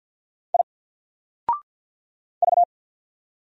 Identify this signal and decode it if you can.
Morse code, heard as a keyed tone, about 50 wpm, 700 Hz. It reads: I V
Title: Single Letter-Number 50wpm